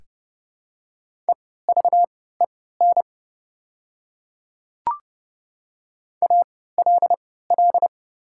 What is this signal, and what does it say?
Morse code, heard as a keyed tone, about 31 wpm, spaced as for 17 wpm, 690 Hz. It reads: EVEN ALL
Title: 3x Character Spacing - Top 200 Words - 30wpm